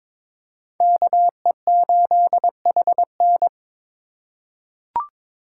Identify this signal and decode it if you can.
Morse code, heard as a keyed tone, about 22 wpm, 700 Hz. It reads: KE8HN
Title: USA Call Signs - 5 Characters - 3x Character Spacing - 22wpm